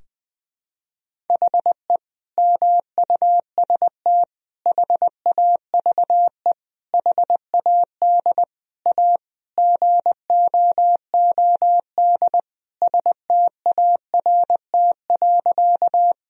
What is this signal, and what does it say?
Morse code, heard as a keyed tone, about 21 wpm, 700 Hz. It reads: HE MUST HAVE HAD A GOOD START.
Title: Sentences from Top 1000 Words - Review Part 1 - 20wpm